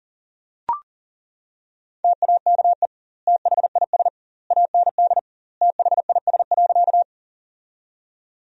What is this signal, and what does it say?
Morse code, heard as a keyed tone, about 40 wpm, 700 Hz. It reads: TAKE THIS AND THIS.
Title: 2x Word-Spacing - Sentences from Top 200 Words - 40wpm